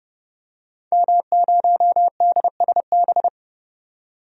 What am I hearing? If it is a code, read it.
Morse code, heard as a keyed tone, about 30 wpm, 700 Hz. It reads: M0DSB
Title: DXCC-100 Call Signs - 5 Characters - 30wpm